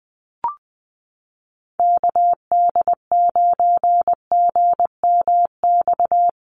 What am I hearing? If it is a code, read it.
Morse code, heard as a keyed tone, about 20 wpm, 700 Hz. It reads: KD9GMX
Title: USA Call Signs Repeated 3x 20wpm